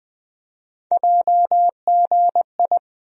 Morse code, heard as a keyed tone, about 20 wpm, 700 Hz. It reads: JGI
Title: USA Call Sign Postfixes - 3 Letters 20wpm